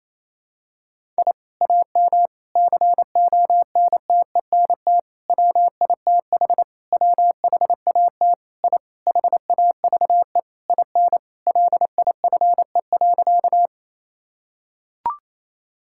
Morse code, heard as a keyed tone, about 28 wpm, 700 Hz. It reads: I AM CONTENT WITH WHAT I HAVE IN LIFE.
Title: Sentences from Top 1000 Words - Review Part 6 - 28wpm